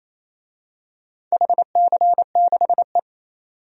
Morse code, heard as a keyed tone, about 28 wpm, 700 Hz. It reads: HC6E
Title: DXCC-100 Call Signs - 4 Characters - 28wpm